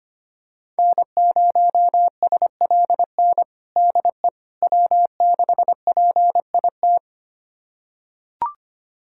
Morse code, heard as a keyed tone, about 25 wpm, 700 Hz. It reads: N0SLN DE W6PIT
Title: USA Callsign DE Callsign - 5 Characters - 25wpm